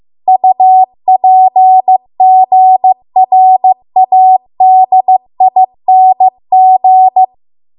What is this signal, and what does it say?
Morse code, heard as a keyed tone, about 15 wpm, 750 Hz. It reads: UPGRADING